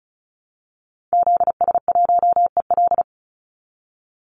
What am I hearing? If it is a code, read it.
Morse code, heard as a keyed tone, about 34 wpm, 700 Hz. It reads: ZS1EL
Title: Call Signs Repeated 2x - 35wpm